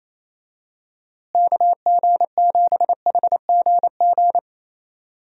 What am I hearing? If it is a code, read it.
Morse code, heard as a keyed tone, about 27 wpm, 700 Hz. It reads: KG7HGG
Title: USA Call Signs - 6 Characters - Repeated 3x - 28wpm